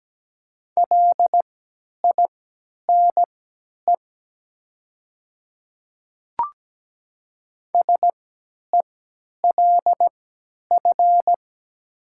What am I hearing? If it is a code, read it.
Morse code, heard as a keyed tone, about 17 wpm, spaced as for 10 wpm, 700 Hz. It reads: LINE SELF
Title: 3x Character Spacing - Top 200 Words - 17wpm